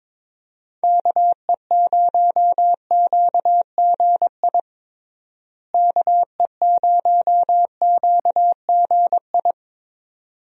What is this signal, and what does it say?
Morse code, heard as a keyed tone, about 21 wpm, 700 Hz. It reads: KE0QGI KE0QGI
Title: USA Call Signs - 6 Characters - Repeated 2x - 22wpm